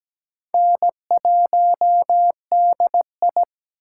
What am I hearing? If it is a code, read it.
Morse code, heard as a keyed tone, about 17 wpm, 700 Hz. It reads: N1DI
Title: USA Call Signs - 4 Characters - 3x Character Spacing - 17wpm